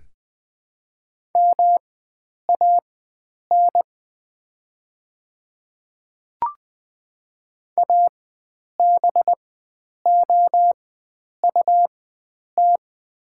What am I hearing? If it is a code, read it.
Morse code, heard as a keyed tone, about 20 wpm, spaced as for 9 wpm, 700 Hz. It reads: MAN ABOUT
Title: 4x Character Spacing - Top 500 Words - 20wpm